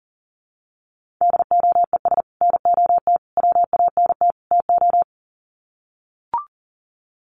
Morse code, heard as a keyed tone, about 40 wpm, 700 Hz. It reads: DOES NOT WANT TO
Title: Sets of 4 Words 40wpm